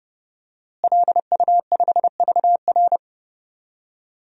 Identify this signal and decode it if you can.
Morse code, heard as a keyed tone, about 30 wpm, 700 Hz. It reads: LU5VR
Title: DXCC-100 Call Signs - 5 Characters - Repeated 3x - 30wpm